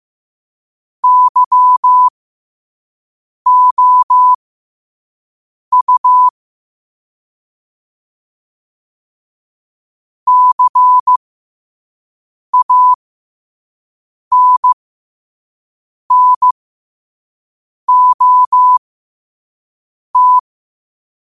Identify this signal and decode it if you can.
Morse code, heard as a keyed tone, about 15 wpm, spaced as for 5 wpm, 1000 Hz. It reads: YOU CANNOT
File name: qotdfiles / 5wpm